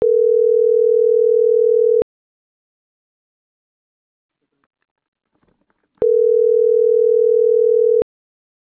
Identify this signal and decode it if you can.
Ringback tone (North America), 440 + 480 Hz.